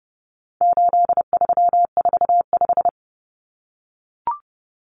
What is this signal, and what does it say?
Morse code, heard as a keyed tone, about 30 wpm, 700 Hz. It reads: 8345
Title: Common Cut Numbers - 4-Digit 30wpm